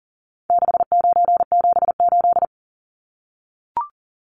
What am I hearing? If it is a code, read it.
Morse code, heard as a keyed tone, about 40 wpm, 700 Hz. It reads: 6978
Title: Numbers - 4-Digit 40wpm